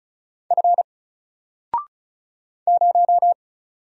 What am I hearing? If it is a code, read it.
Morse code, heard as a keyed tone, about 36 wpm, 700 Hz. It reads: F 0
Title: Single Character (Letter:Number:Punctuation:Prosign) Alt 35wpm